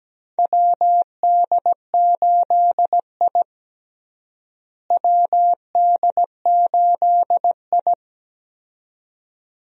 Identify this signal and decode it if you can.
Morse code, heard as a keyed tone, about 17 wpm, 700 Hz. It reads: WD8I WD8I
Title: USA Call Signs - 4 Characters - Repeated 2x - 17wpm